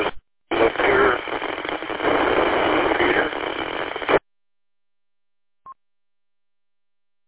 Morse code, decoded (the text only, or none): none